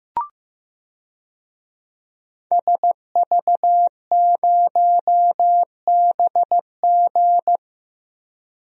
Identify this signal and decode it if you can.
Morse code, heard as a keyed tone, about 15 wpm, 700 Hz. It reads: SV0BG
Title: DXCC-100 Call Signs - 5 Characters - Repeated 3x - 15wpm